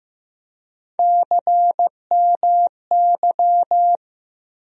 Morse code, heard as a keyed tone, about 15 wpm, 700 Hz. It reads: CMY
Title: USA Call Sign Postfixes - 3 Letters 15wpm